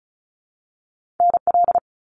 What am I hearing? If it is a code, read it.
Morse code, heard as a keyed tone, about 36 wpm, 700 Hz. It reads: NL
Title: Canadian Provinces and Territories - Repeated 3x - 35wpm